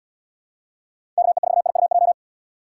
Morse code, heard as a keyed tone, about 76 wpm, 700 Hz. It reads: KE4IUJ